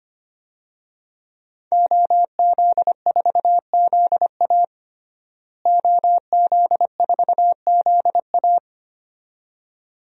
Morse code, heard as a keyed tone, about 25 wpm, 700 Hz. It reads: OZ4ZA OZ4ZA
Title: DXCC-100 Call Signs - 5 Characters - Repeated 2x - 25wpm